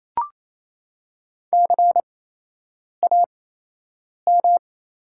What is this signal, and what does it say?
Morse code, heard as a keyed tone, about 27 wpm, spaced as for 8 wpm, 700 Hz. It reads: CAM
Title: 8x Character Spacing - Top 200 Words - 28wpm